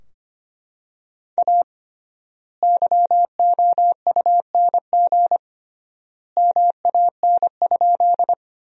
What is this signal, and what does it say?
Morse code, heard as a keyed tone, about 25 wpm, 700 Hz. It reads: A YOUNG MAN?
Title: 3x Word-Spacing - Sentences from Top 400 Words - 25wpm